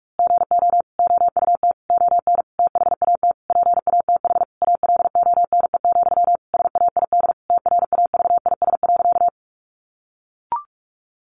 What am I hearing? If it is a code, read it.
Morse code, heard as a keyed tone, about 44 wpm, 700 Hz. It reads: GO OUT ON THAT PATH ALONE, SAID TRAVIS.
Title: A Sound Of Thunder 45wpm